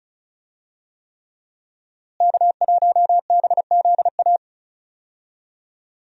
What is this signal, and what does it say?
Morse code, heard as a keyed tone, about 36 wpm, 700 Hz. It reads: K1BZA
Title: USA Call Signs - 5 Characters - Repeated 3x - 35wpm